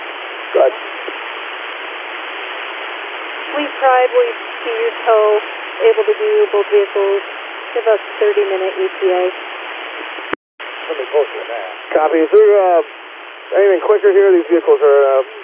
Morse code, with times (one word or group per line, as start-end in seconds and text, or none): none